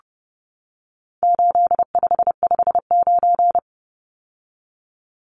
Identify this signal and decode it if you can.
Morse code, heard as a keyed tone, about 30 wpm, 700 Hz. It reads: 8559
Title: Numbers - 4-Digit 30wpm